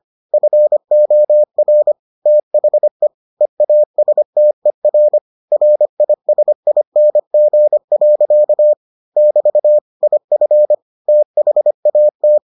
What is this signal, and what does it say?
Morse code, heard as a keyed tone, about 25 wpm, 600 Hz. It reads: FOR THE EASTER RISING. <BT> IF THAT